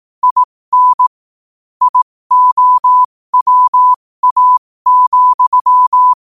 Morse code, heard as a keyed tone, about 18 wpm, spaced as for 15 wpm, 1000 Hz. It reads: IN IOWA,
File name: qotdfiles / 15wpm